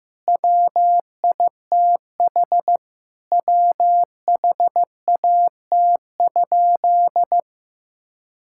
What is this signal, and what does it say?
Morse code, heard as a keyed tone, about 15 wpm, 700 Hz. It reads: WITH WHAT?
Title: Sentences from Top 1600 Words 15wpm